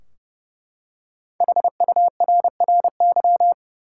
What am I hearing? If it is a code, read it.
Morse code, heard as a keyed tone, about 32 wpm, 700 Hz. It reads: HURRY
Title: Five Letter Words - 30wpm